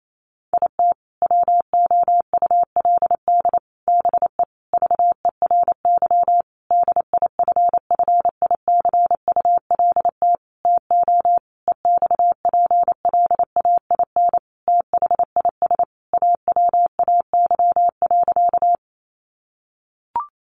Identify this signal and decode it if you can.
Morse code, heard as a keyed tone, about 28 wpm, 700 Hz. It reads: IT WOULD BE VERY DIFFICULT TO EXPLAIN THIS AWAY.
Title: Sentences from Top 1100 Words 28wpm